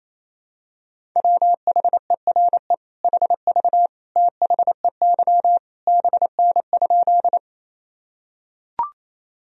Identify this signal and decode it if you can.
Morse code, heard as a keyed tone, about 28 wpm, 700 Hz. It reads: WHERE HV THEY BN?
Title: Sentences from Top 200 Words and Abbreviations 28wpm